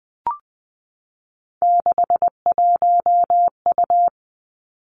Morse code, heard as a keyed tone, about 20 wpm, 700 Hz. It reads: 61U
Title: Three Challenging Letter-Number Combinations 20wpm